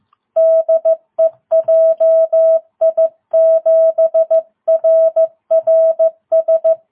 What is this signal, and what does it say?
Morse code, heard as a keyed tone, about 14 wpm, 660 Hz. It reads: DE JI7RRS
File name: jh8cbh / 2022cw